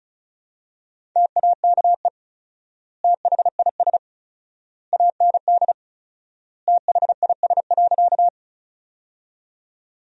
Morse code, heard as a keyed tone, about 36 wpm, 700 Hz. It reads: TAKE THIS AND THIS.
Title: 4x Word-Spacing - Sentences from Top 200 Words - 35wpm